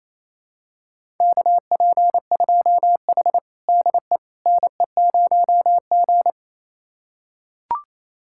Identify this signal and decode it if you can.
Morse code, heard as a keyed tone, about 27 wpm, 700 Hz. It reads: KP2H DE NE0G
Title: USA Callsign DE Callsign - 4 Characters - 28wpm